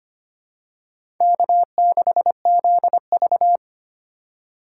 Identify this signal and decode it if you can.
Morse code, heard as a keyed tone, about 25 wpm, 700 Hz. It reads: K6ZV
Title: USA Call Signs - 4 Characters - 2x Character Spacing - 25wpm